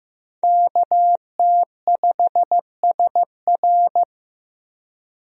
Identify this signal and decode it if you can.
Morse code, heard as a keyed tone, about 15 wpm, 700 Hz. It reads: KT5SR
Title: USA Call Signs - 5 Characters - 15wpm